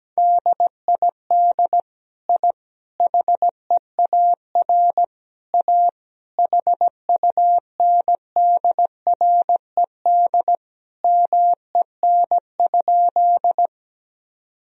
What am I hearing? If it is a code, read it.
Morse code, heard as a keyed tone, about 17 wpm, 700 Hz. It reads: DID I HEAR A HUNDRED MEN?
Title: Sentences from Top 1000 Words - Review Part 2 - 17wpm